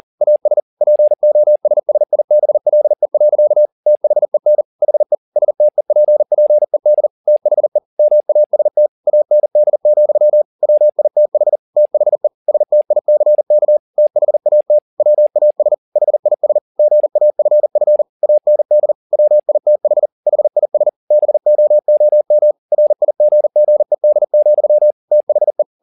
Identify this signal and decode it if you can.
Morse code, heard as a keyed tone, about 40 wpm, 600 Hz. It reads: AS POSSIBLE. THEN HE STEPPED THE MAST AND, WITH THE STICK THAT WAS HIS GAFF AND WITH HIS BOOM RIGGED, THE
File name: the old man and the sea 2 40wpm